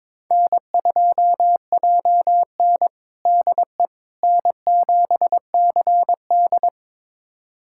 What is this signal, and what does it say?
Morse code, heard as a keyed tone, about 22 wpm, 700 Hz. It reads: N2JN DE N7CD
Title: USA Callsign DE Callsign - 4 Characters - 22wpm